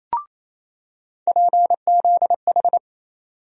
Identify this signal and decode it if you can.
Morse code, heard as a keyed tone, about 28 wpm, 700 Hz. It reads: PZH
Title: USA Call Sign Postfixes - 3 Letters 28wpm